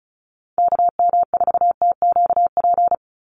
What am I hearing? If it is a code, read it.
Morse code, heard as a keyed tone, about 34 wpm, 700 Hz. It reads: KM4TQP
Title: USA Call Signs 35wpm